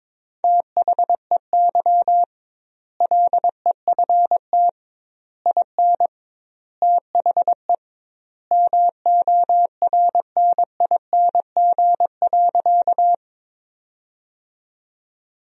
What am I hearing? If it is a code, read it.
Morse code, heard as a keyed tone, about 22 wpm, 700 Hz. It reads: THEY LEFT IN THE MORNING.
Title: 2x Word-Spacing - Sentences from Top 400 Words - 22wpm